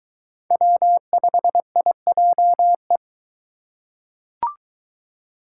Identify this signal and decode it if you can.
Morse code, heard as a keyed tone, about 23 wpm, 700 Hz. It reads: W5IJE